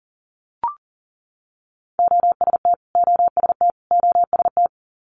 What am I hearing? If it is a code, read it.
Morse code, heard as a keyed tone, about 41 wpm, 700 Hz. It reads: OST OST OST
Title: Multi-letter Phonemes Repeated 3x 40wpm